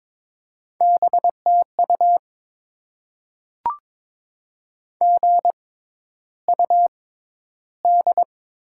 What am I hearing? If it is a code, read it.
Morse code, heard as a keyed tone, about 22 wpm, spaced as for 11 wpm, 700 Hz. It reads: BTU GUD
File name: Top 20 QSO Elements - 6x Character Spacing - 22wpm